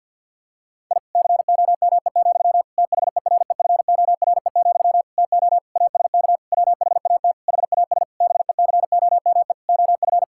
Morse code, heard as a keyed tone, about 52 wpm, 700 Hz. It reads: I COME, THEREFORE, TO ASK WHAT HAS BECOME OF
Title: Sentences from Top 1000 Words - Review Part 5 - 50wpm